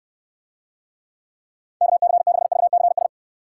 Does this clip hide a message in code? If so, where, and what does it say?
Yes: Morse code from 1.8 to 3.1 s, heard as a keyed tone, about 68 wpm, 700 Hz. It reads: KC6VCS